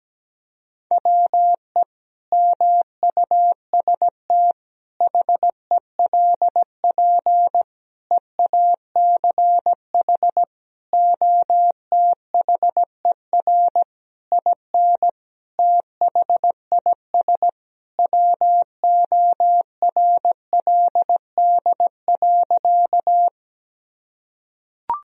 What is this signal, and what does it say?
Morse code, heard as a keyed tone, about 17 wpm, 700 Hz. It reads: WE MUST HELP EACH OTHER IN THIS WORLD.
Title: Sentences from Top 1000 Words - Review Part 1 - 17wpm